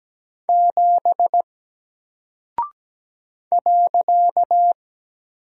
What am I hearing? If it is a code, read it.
Morse code, heard as a keyed tone, about 17 wpm, 700 Hz. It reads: 7 .
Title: Single Character (Letter:Number:Punctuation:Prosign) 17wpm